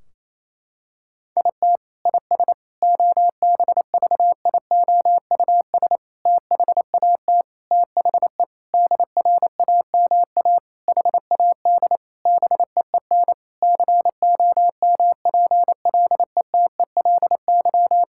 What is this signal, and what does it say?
Morse code, heard as a keyed tone, about 28 wpm, 700 Hz. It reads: IT IS OBVIOUS THAT THE DRAMA HAD BEEN COMPLETELY
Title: Sentences from Top 1600 Words 28wpm